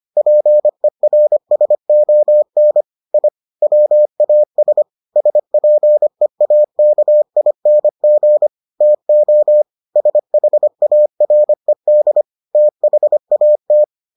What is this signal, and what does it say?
Morse code, heard as a keyed tone, about 25 wpm, 600 Hz. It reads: PERSON I WAS SPEAKING TO SHARED THAT